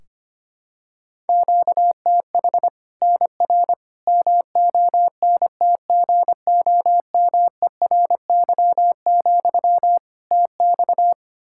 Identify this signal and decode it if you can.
Morse code, heard as a keyed tone, about 25 wpm, 700 Hz. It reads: QTH NR MONTGOMERY, TX
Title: Practice QSO - Exchange 1 - Beginning - 25wpm